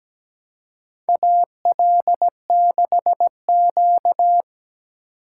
Morse code, heard as a keyed tone, about 17 wpm, 700 Hz. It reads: AL6Q
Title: USA Call Signs - 4 Characters - 8x Character Spacing - 17wpm